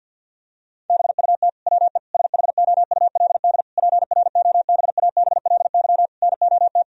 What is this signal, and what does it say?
Morse code, heard as a keyed tone, about 46 wpm, 700 Hz. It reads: BUT WE SHOULD PROBABLY NOT
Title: Sentences from Top 1000 Words - Review Part 4 - 50wpm